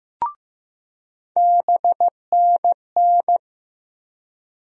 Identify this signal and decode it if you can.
Morse code, heard as a keyed tone, about 15 wpm, 700 Hz. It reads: BNN
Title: USA Call Sign Postfixes - 3 Letters 15wpm